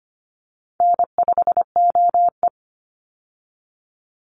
Morse code, heard as a keyed tone, about 25 wpm, 700 Hz. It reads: N5OE
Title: Easy Call Signs - 25wpm